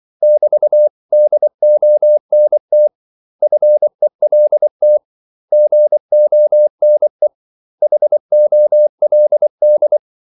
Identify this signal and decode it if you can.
Morse code, heard as a keyed tone, about 24 wpm, 600 Hz. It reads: <BT>DONT FELT GONE HOLD
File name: CW / Practice / 500MostCommonEnglishWords24